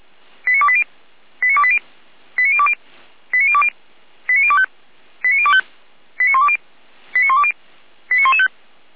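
Five-tone selective call (ZVEI), heard as quick runs of five tones, tones of about 70 ms. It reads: 89280 89280 89020 89020 89025 89025 89120 89120 89115
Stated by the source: ZVEI five-tone selective call received by SDR, served from audio.f85360.feuerwehr-orlamuende.org (Feuerwehr Orlamünde).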